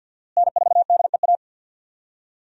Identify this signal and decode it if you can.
Morse code, heard as a keyed tone, about 49 wpm, 700 Hz. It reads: N4DEA
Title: USA Call Signs - 5 Characters - Repeated 3x - 50wpm